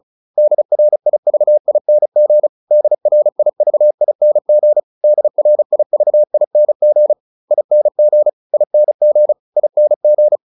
Morse code, heard as a keyed tone, about 34 wpm, 600 Hz. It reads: DRIVING DRIVING DRIVING ING ING ING
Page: advanced-practice-files